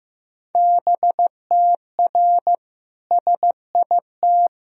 Morse code, heard as a keyed tone, about 15 wpm, 700 Hz. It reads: BTR SIT
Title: Sentences from Top 500 Words and Abbreviations 15wpm